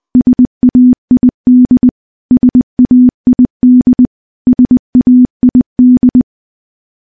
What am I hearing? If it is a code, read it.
Morse code, heard as a keyed tone, about 20 wpm, 250 Hz. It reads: SAID SAID SAID
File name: morse-training-20WPM